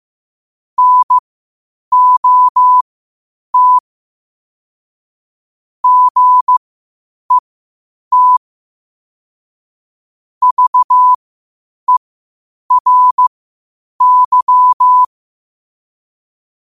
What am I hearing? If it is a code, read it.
Morse code, heard as a keyed tone, about 15 wpm, spaced as for 8 wpm, 1000 Hz. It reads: NOT GET VERY
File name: qotdfiles / 8wpm